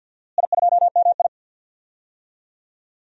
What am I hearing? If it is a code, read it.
Morse code, heard as a keyed tone, about 46 wpm, 700 Hz. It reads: I2MI
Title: DXCC-100 Call Signs - 4 Characters - Repeated 2x - 50wpm